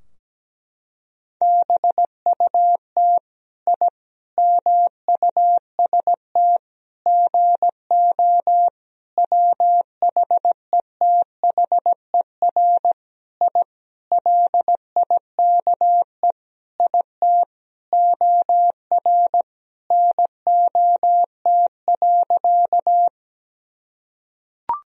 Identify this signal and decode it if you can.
Morse code, heard as a keyed tone, about 17 wpm, 700 Hz. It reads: BUT I MUST GO WHETHER I LIKE IT OR NOT.
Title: Sentences from Top 1000 Words - Review Part 3 - 17wpm